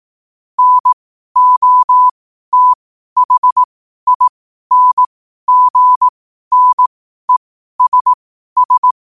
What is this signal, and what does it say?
Morse code, heard as a keyed tone, about 18 wpm, spaced as for 12 wpm, 1000 Hz. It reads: NOTHINGNESS
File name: qotdfiles / 12wpm